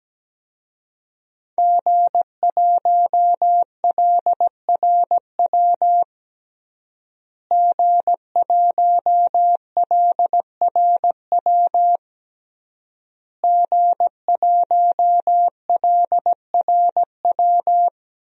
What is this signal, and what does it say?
Morse code, heard as a keyed tone, about 17 wpm, 700 Hz. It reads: G1LRW G1LRW G1LRW
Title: DXCC-100 Call Signs - 5 Characters - Repeated 3x - 17wpm